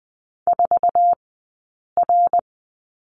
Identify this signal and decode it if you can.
Morse code, heard as a keyed tone, about 20 wpm, 700 Hz. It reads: 4 R
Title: 059 Lesson 11d - Letter h - Two Characters 20wpm